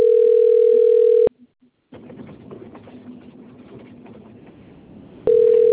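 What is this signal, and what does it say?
Ringback tone (North America), 440 + 480 Hz.
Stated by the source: Outgoing phone call recorded on the Rio Claro 2 system.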